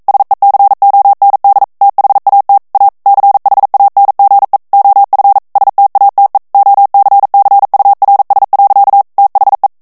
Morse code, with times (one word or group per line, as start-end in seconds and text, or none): 0.1-1.6 SECOND
1.8-2.6 THAT
2.7-2.9 A
3.1-4.6 CHANGE
4.7-5.4 OF
5.6-6.4 STATE
6.5-9.0 OCCURS.
9.2-9.7 THE